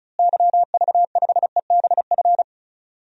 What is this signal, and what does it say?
Morse code, heard as a keyed tone, about 36 wpm, 700 Hz. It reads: YV5EBF
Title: DXCC-100 Call Signs - 6 Characters - Repeated 2x - 35wpm